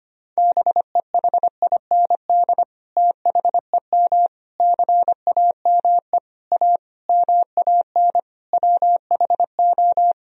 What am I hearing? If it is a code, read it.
Morse code, heard as a keyed tone, about 25 wpm, 700 Hz. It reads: BEHIND THEM CAME A MAN WHO
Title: Sentences from Top 1000 Words - Review Part 4 - 25wpm